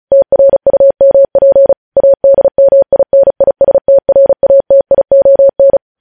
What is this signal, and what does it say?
Morse code, heard as a keyed tone, about 36 wpm, 550 Hz. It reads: TRUMP ADMINISTRATION
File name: podcast / News headlines-1x-35x35